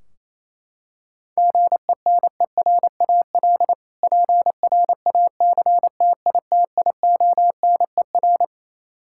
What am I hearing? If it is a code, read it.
Morse code, heard as a keyed tone, about 28 wpm, 700 Hz. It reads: GENERAL PRACTITIONER
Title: Practice QSO - Exchange 3 - More Info - 28wpm